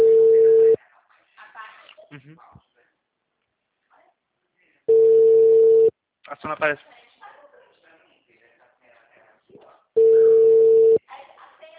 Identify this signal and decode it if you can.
Ringback tone (China, 450 Hz), with speech.